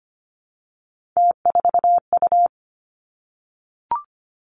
Morse code, heard as a keyed tone, about 25 wpm, 700 Hz. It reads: T4U
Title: Three Challenging Letter-Number Combinations 25wpm